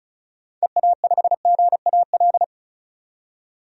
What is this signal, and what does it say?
Morse code, heard as a keyed tone, about 36 wpm, 700 Hz. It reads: EA5GAL
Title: DXCC-100 Call Signs - 6 Characters - 35wpm